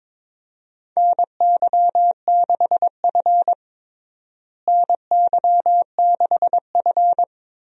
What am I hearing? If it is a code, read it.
Morse code, heard as a keyed tone, about 22 wpm, 700 Hz. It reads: NY6F NY6F
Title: USA Call Signs - 4 Characters - Repeated 3x - 22wpm